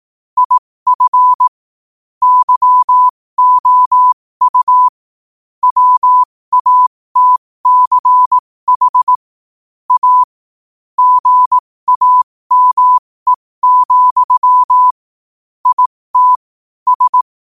Morse code, heard as a keyed tone, about 18 wpm, spaced as for 15 wpm, 1000 Hz. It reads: IF YOU WATCH A GAME, IT S
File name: qotdfiles / 15wpm